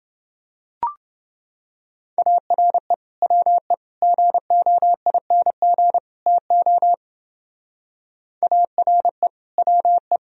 Morse code, heard as a keyed tone, about 28 wpm, 700 Hz. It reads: ARE WE GOING TO ARE WE